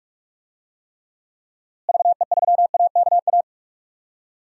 Morse code, heard as a keyed tone, about 46 wpm, 700 Hz. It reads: VE3AKA
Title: DXCC-100 Call Signs - 6 Characters - Repeated 3x - 45wpm